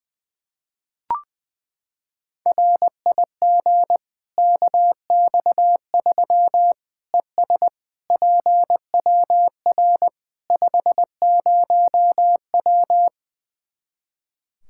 Morse code, heard as a keyed tone, about 21 wpm, 700 Hz. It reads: RIG KX3 ES PWR 50W
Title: Practice QSO - Exchange 2 - Info - 20wpm